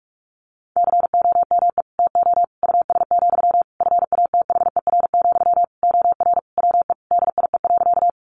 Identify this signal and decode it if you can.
Morse code, heard as a keyed tone, about 44 wpm, 700 Hz. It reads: COME TO US, FATHER, OR WE DIE.
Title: Sentences from Top 1300 Words 45wpm